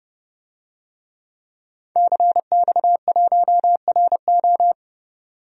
Morse code, heard as a keyed tone, about 30 wpm, 700 Hz. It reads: CX1RO
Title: DXCC-100 Call Signs - 5 Characters - 30wpm